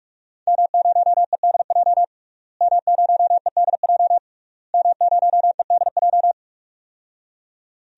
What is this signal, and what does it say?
Morse code, heard as a keyed tone, about 41 wpm, 700 Hz. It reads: M0EDJ M0EDJ M0EDJ
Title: DXCC-100 Call Signs - 5 Characters - Repeated 3x - 45wpm